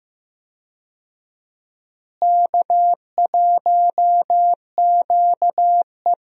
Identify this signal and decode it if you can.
Morse code, heard as a keyed tone, about 15 wpm, 700 Hz. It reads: K1QE
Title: USA Call Signs - 4 Characters - Repeated 2x - 15wpm